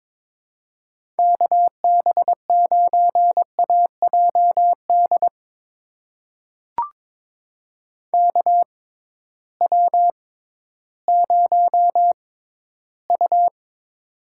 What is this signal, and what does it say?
Morse code, heard as a keyed tone, about 21 wpm, spaced as for 13 wpm, 700 Hz. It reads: KB9AJD KW0U
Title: USA Call Signs - 6x Character Spacing - 22wpm